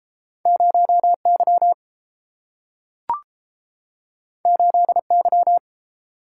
Morse code, heard as a keyed tone, about 30 wpm, 700 Hz. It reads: 0Y 8Y